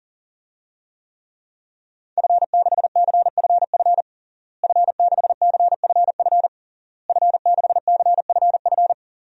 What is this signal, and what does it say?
Morse code, heard as a keyed tone, about 40 wpm, 700 Hz. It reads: F6CFF F6CFF F6CFF
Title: DXCC-100 Call Signs - 5 Characters - Repeated 3x - 40wpm